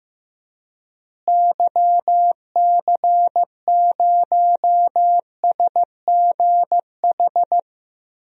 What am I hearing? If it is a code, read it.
Morse code, heard as a keyed tone, about 16 wpm, 700 Hz. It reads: YC0SGH
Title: DXCC-100 Call Signs - 6 Characters - Repeated 3x - 15wpm